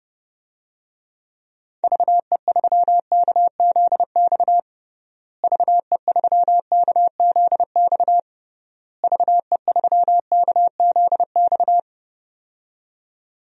Morse code, heard as a keyed tone, about 30 wpm, 700 Hz. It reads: VE3KZX VE3KZX VE3KZX
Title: DXCC-100 Call Signs - 6 Characters - Repeated 3x - 30wpm